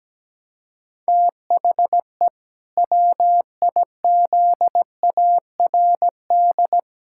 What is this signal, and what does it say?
Morse code, heard as a keyed tone, about 17 wpm, 700 Hz. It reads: THE WIZARD
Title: Pangram Challenge v2 - 17wpm